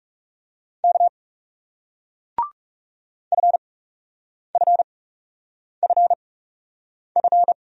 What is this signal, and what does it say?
Morse code, heard as a keyed tone, about 38 wpm, spaced as for 8 wpm, 700 Hz. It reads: K FFFF